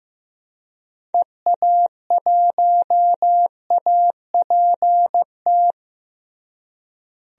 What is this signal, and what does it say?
Morse code, heard as a keyed tone, about 15 wpm, 700 Hz. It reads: EA1APT
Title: DXCC-100 Call Signs - 6 Characters - Repeated 3x - 15wpm